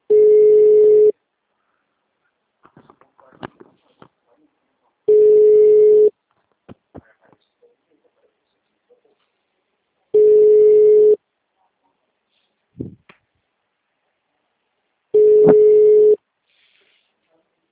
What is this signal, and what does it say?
Ringback tone (Europe, 425 Hz).
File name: out- -2014- - - .58286